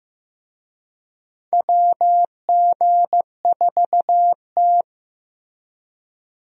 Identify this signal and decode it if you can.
Morse code, heard as a keyed tone, about 15 wpm, 700 Hz. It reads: WG4T
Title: USA Call Signs - 4 Characters - Repeated 2x - 15wpm